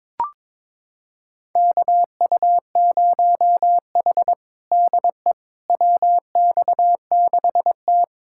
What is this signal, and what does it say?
Morse code, heard as a keyed tone, about 22 wpm, 700 Hz. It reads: KU0H DE WX6T
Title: USA Callsign DE Callsign - 4 Characters - 22wpm